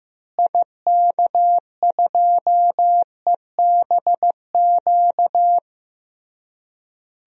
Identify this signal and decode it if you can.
Morse code, heard as a keyed tone, about 15 wpm, 700 Hz. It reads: IK2EBQ
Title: DXCC-100 Call Signs - 6 Characters - Repeated 3x - 15wpm